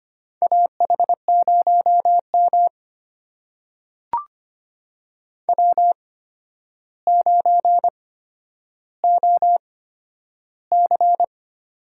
Morse code, heard as a keyed tone, about 25 wpm, spaced as for 11 wpm, 700 Hz. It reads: AH0M W9OC
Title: USA Call Signs - 4 Characters - 8x Character Spacing - 25wpm